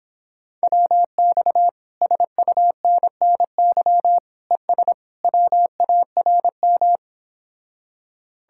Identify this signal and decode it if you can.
Morse code, heard as a keyed tone, about 29 wpm, 700 Hz. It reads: WX SUNNY ES WARM